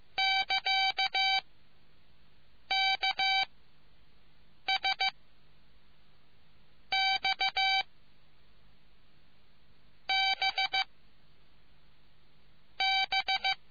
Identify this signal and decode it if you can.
Morse code, heard as a keyed tone, about 15 wpm, 770 Hz. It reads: <KA> K S X B B